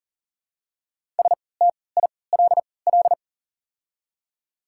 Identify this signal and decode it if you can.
Morse code, heard as a keyed tone, about 40 wpm, spaced as for 23 wpm, 700 Hz. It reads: STILL
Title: 3x Character Spacing - Top 500 Words - 40wpm